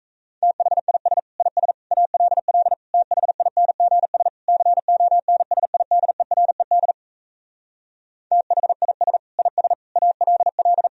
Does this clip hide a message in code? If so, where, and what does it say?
Yes: Morse code from 0.4 to 10.9 s, heard as a keyed tone, about 38 wpm, 700 Hz. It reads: THIS IS ALL THINGS CONSIDERED THIS IS ALL